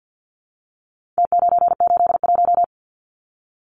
Morse code, heard as a keyed tone, about 50 wpm, 700 Hz. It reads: T981
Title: Numbers - Contest Style - 4-Digit 50wpm